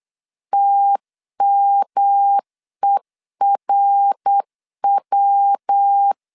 Morse code, heard as a keyed tone, about 8 wpm, 800 Hz. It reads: TMERW
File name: CW / DET / 1 3 1500C S C 8wpm 4214